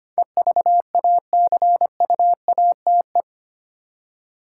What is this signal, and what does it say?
Morse code, heard as a keyed tone, about 25 wpm, 700 Hz. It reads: EVACUATE
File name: 8 Letter Words - First 3 Letters Given - 25wpm